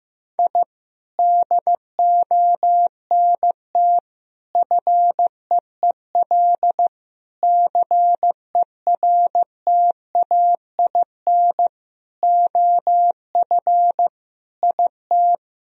Morse code, heard as a keyed tone, about 15 wpm, 700 Hz. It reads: I DONT FEEL CERTAIN OF IT
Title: Sentences from Top 1000 Words - Review Part 2 - 15wpm